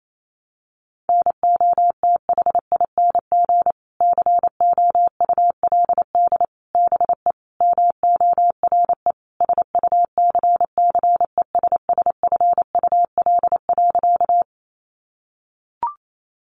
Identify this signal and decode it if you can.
Morse code, heard as a keyed tone, about 28 wpm, 700 Hz. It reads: NOTHING COULD BE MORE SUCCESSFUL.
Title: Sentences from Top 900 Words 28wpm